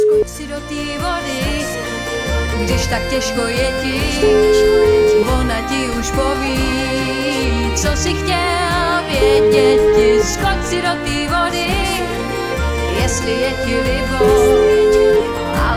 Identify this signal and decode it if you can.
Ringback tone (Europe, 425 Hz), with speech.